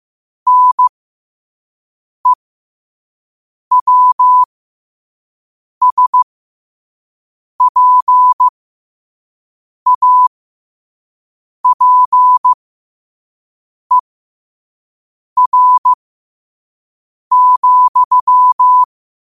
Morse code, heard as a keyed tone, about 15 wpm, spaced as for 5 wpm, 1000 Hz. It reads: NEWSPAPER,